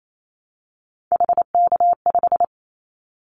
Morse code, heard as a keyed tone, about 28 wpm, 700 Hz. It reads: HK5
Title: Callsign Prefixes for Top 100 DXCC - 28wpm